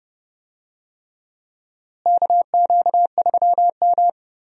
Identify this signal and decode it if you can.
Morse code, heard as a keyed tone, about 31 wpm, 700 Hz. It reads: KQ3M
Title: USA Call Signs - 4 Characters - Repeated 3x - 30wpm